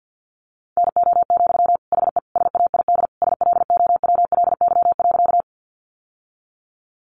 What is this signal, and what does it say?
Morse code, heard as a keyed tone, about 48 wpm, 700 Hz. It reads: NO, HE SAID SLOWLY.
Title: Sentences from Top 1100 Words 50wpm